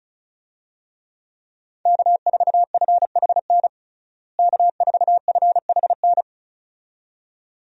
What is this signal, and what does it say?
Morse code, heard as a keyed tone, about 36 wpm, 700 Hz. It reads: K4FHN K4FHN
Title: USA Call Signs - 5 Characters - Repeated 2x - 35wpm